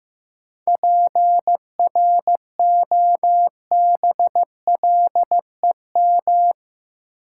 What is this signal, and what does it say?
Morse code, heard as a keyed tone, about 15 wpm, 700 Hz. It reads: PROBLEM
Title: Sentences from Top 1000 Words - New Word Drill - 15wpm